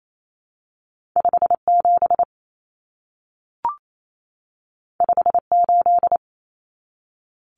Morse code, heard as a keyed tone, about 28 wpm, 700 Hz. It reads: 57 58
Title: Numbers - 2-Digit 28wpm